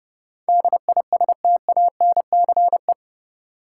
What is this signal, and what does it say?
Morse code, heard as a keyed tone, about 30 wpm, 700 Hz. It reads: DISTANCE
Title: Sentences from Top 1200 Words - New Word Drill Alt - 30wpm